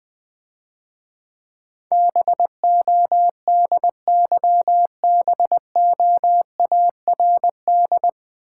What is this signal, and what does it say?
Morse code, heard as a keyed tone, about 20 wpm, 700 Hz. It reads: BODYBOARD
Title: Difficult Words - Speed-Racing with Hints - 20wpm